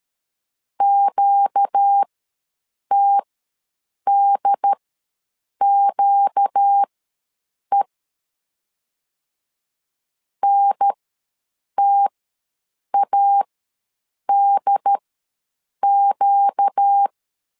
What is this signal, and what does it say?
Morse code, heard as a keyed tone, about 13 wpm, spaced as for 7 wpm, 800 Hz. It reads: QTDQE NTADQ
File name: LEZIONI / 4 Q ws 12wpm 1912